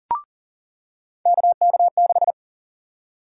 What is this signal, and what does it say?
Morse code, heard as a keyed tone, about 40 wpm, 700 Hz. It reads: KK6
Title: USA Call Sign Prefixes - 40wpm